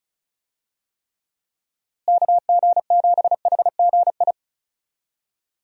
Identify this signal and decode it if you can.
Morse code, heard as a keyed tone, about 34 wpm, 700 Hz. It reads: KG7HGI
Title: USA Call Signs - 6 Characters - Repeated 2x - 35wpm